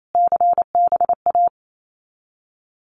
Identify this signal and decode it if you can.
Morse code, heard as a keyed tone, about 28 wpm, 700 Hz. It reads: CBA
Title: Abbreviations Repeated 3x - 28wpm